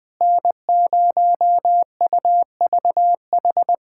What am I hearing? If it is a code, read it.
Morse code, heard as a keyed tone, about 20 wpm, 700 Hz. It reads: N0UVH
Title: USA Callsign DE Callsign - 5 Characters - 20wpm